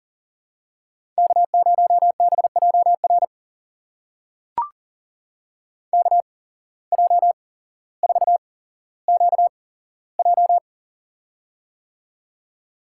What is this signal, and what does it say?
Morse code, heard as a keyed tone, about 40 wpm, spaced as for 17 wpm, 700 Hz. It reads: K0BJR KJ4QJ
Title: USA Call Signs - 5 Characters - 8x Character Spacing - 40wpm